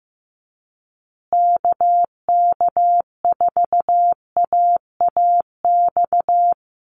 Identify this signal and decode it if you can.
Morse code, heard as a keyed tone, about 15 wpm, 700 Hz. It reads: KK4AAX
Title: USA Call Signs Repeated 3x 15wpm